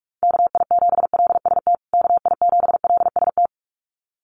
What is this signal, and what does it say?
Morse code, heard as a keyed tone, about 44 wpm, 700 Hz. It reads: KI7LST KI7LST
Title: USA Call Signs Repeated 2x 45wpm